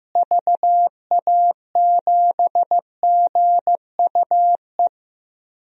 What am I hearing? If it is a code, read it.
Morse code, heard as a keyed tone, about 15 wpm, 700 Hz. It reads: VA7GUE
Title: DXCC-100 Call Signs - 6 Characters - Repeated 3x - 15wpm